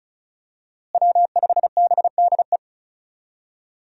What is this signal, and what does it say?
Morse code, heard as a keyed tone, about 34 wpm, 700 Hz. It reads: W5BDE
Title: USA Call Signs - 5 Characters - Repeated 2x - 35wpm